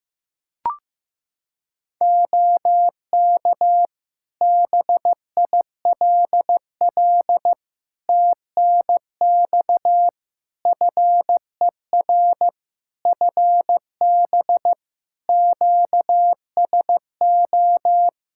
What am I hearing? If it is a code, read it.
Morse code, heard as a keyed tone, about 14 wpm, 700 Hz. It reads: OK BILL TNX FER FB QSO
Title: Practice QSO - Exchange 4 - Ending - 15wpm